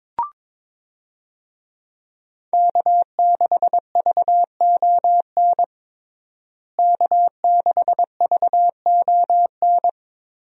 Morse code, heard as a keyed tone, about 22 wpm, 700 Hz. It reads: K6VON K6VON
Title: USA Call Signs - 5 Characters - Repeated 3x - 22wpm